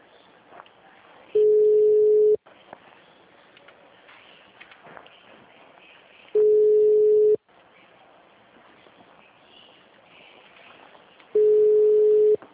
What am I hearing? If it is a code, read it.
Ringback tone (Europe, 425 Hz).